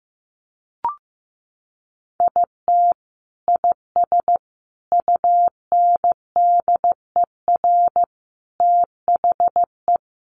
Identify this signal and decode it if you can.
Morse code, heard as a keyed tone, about 15 wpm, 700 Hz. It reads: IT IS UNDER THE
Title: Sentences from Top 1400 Words 15wpm